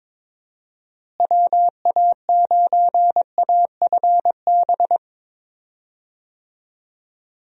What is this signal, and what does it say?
Morse code, heard as a keyed tone, about 22 wpm, 700 Hz. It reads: WA9AFB
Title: USA Call Signs - 6 Characters - Repeated 3x - 22wpm